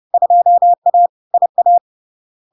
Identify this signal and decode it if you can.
Morse code, heard as a keyed tone, about 30 wpm, 700 Hz. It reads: 2A IA
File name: LARK-FD-2022-Log-in-CW-30wpm-700-Hz